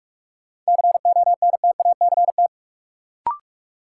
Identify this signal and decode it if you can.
Morse code, heard as a keyed tone, about 46 wpm, 700 Hz. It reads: CONTACT
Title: Sentences from Top 1000 Words - Review Part 7 - New Word Drill - 45wpm